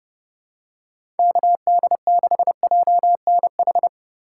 Morse code, heard as a keyed tone, about 30 wpm, 700 Hz. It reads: KD6JNH
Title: USA Call Signs - 6 Characters - 2x Character Spacing - 30wpm